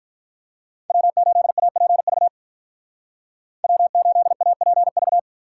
Morse code, heard as a keyed tone, about 44 wpm, 700 Hz. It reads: W8APV W8APV